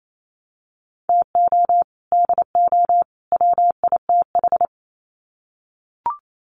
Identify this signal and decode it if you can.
Morse code, heard as a keyed tone, about 28 wpm, 700 Hz. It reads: TO DO WITH
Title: Sets of 3 Words 28wpm